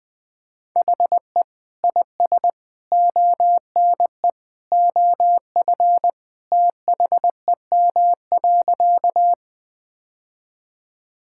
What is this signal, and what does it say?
Morse code, heard as a keyed tone, about 21 wpm, 700 Hz. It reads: HE IS ONE OF THEM.
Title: Sentences with 3-Word Phrases - 20wpm